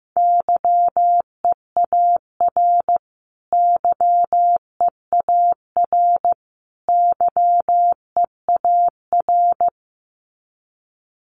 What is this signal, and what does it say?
Morse code, heard as a keyed tone, about 15 wpm, 700 Hz. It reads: YEAR YEAR YEAR
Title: Top 500 Words Repeated 15wpm